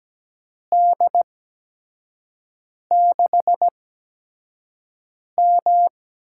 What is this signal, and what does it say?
Morse code, heard as a keyed tone, about 17 wpm, spaced as for 5 wpm, 700 Hz. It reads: D6M
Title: USA Call Signs - 4 Characters - 8x Character Spacing - 17wpm